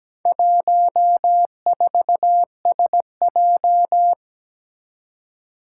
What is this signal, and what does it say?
Morse code, heard as a keyed tone, about 17 wpm, 700 Hz. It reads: 14SJ
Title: Four Challenging Letter-Number Combinations 17wpm